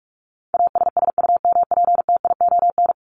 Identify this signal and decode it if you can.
Morse code, heard as a keyed tone, about 44 wpm, 700 Hz. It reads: ASSUMPTION
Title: Top 1751-2000 Words 45wpm